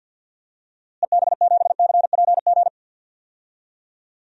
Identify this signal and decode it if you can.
Morse code, heard as a keyed tone, about 52 wpm, 700 Hz. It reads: EB7CPG
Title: DXCC-100 Call Signs - 6 Characters - Repeated 2x - 50wpm